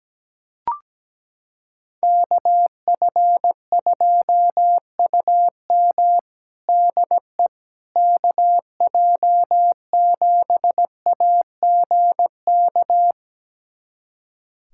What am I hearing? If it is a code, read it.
Morse code, heard as a keyed tone, about 17 wpm, 700 Hz. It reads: KF2UM DE KJ7AGK
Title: USA Callsign DE Callsign 17wpm